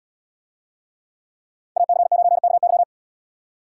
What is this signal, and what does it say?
Morse code, heard as a keyed tone, about 69 wpm, 690 Hz. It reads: AC9KY